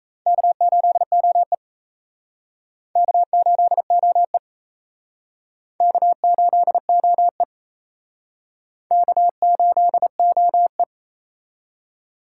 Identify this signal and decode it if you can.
Morse code, heard as a keyed tone, about 36 wpm, 700 Hz. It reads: K8OE K8OE K8OE K8OE